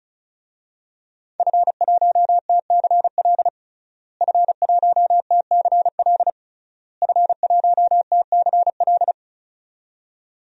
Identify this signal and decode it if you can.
Morse code, heard as a keyed tone, about 36 wpm, 700 Hz. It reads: F1TCL F1TCL F1TCL
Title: DXCC-100 Call Signs - 5 Characters - Repeated 3x - 35wpm